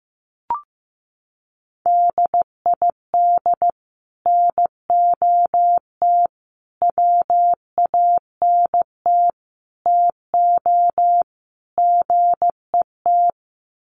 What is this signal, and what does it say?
Morse code, heard as a keyed tone, about 15 wpm, 700 Hz. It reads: DID NOT WANT TO GET
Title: Sets of 5 Words Encore2 15wpm